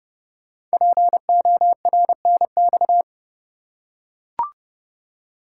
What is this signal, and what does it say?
Morse code, heard as a keyed tone, about 28 wpm, 700 Hz. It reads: PORNX
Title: Code Groups - 5 Letters - 30wpm